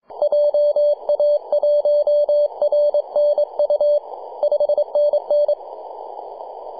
Morse code, heard as a keyed tone, about 25 wpm, 590 Hz. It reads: JA1RNU 5NN
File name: 3d2ya-18-085-cw